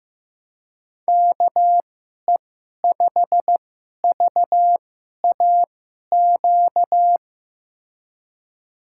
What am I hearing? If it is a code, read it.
Morse code, heard as a keyed tone, about 15 wpm, spaced as for 11 wpm, 700 Hz. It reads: KE5VAQ
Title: USA Call Signs - 6 Characters - 2x Character Spacing - 15wpm